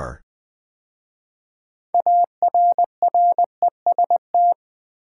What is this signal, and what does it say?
Morse code, heard as a keyed tone, about 20 wpm, 700 Hz. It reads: ARREST
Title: 6 Letter Words - First 3 Letters Given - 20wpm